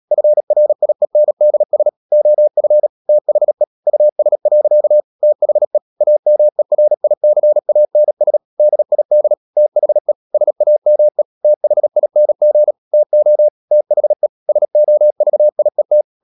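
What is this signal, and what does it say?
Morse code, heard as a keyed tone, about 37 wpm, 600 Hz. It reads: FRIENDS OF THE US. THE AMERICANS DID THE SAME THING TO THE SOVIET
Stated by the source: Morse code practice at 35–39 wpm sending English text of Ken Follett's novel Never, from CW med Gnister.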